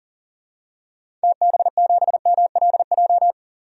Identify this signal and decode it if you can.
Morse code, heard as a keyed tone, about 37 wpm, 700 Hz. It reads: TB7MLJ
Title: DXCC-100 Call Signs - 6 Characters - 40wpm